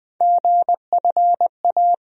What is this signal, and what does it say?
Morse code, heard as a keyed tone, about 20 wpm, 700 Hz. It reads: GFA
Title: USA Call Sign Postfixes - 3 Letters 20wpm